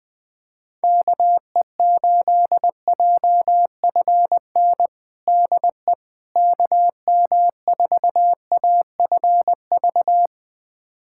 Morse code, heard as a keyed tone, about 20 wpm, 700 Hz. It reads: KE8JFN DE KM4AFV
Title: USA Callsign DE Callsign - 6 Characters - 20wpm